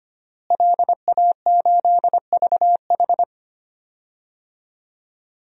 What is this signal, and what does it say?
Morse code, heard as a keyed tone, about 25 wpm, 700 Hz. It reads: LA8VH
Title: DXCC-100 Call Signs - 5 Characters - Repeated 2x - 25wpm